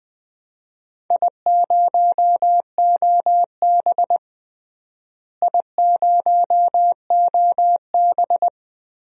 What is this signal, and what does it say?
Morse code, heard as a keyed tone, about 20 wpm, 700 Hz. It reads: I0OB I0OB
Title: DXCC-100 Call Signs - 4 Characters - Repeated 3x - 20wpm